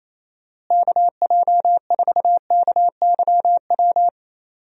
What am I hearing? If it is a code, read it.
Morse code, heard as a keyed tone, about 28 wpm, 700 Hz. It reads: KJ4KYW
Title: USA Call Signs - 6 Characters - Repeated 3x - 28wpm